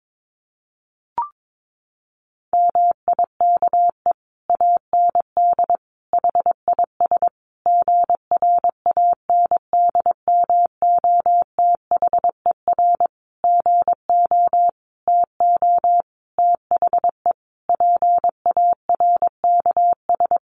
Morse code, heard as a keyed tone, about 22 wpm, 700 Hz. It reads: MIKE AND HIS GRANDMOTHER GO TO THE PARKS
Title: Additional Easy Phrases 22wpm